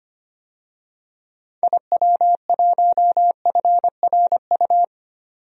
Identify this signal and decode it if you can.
Morse code, heard as a keyed tone, about 25 wpm, 700 Hz. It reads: IW1FRU
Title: DXCC-100 Call Signs - 6 Characters - 25wpm